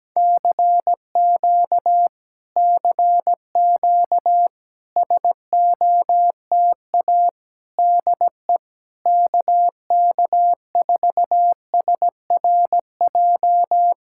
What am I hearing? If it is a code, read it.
Morse code, heard as a keyed tone, about 17 wpm, 700 Hz. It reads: CQ CQ SOTA DE KK4SRJ
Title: Speed-Racing Common QSO Phrases Encore - 17wpm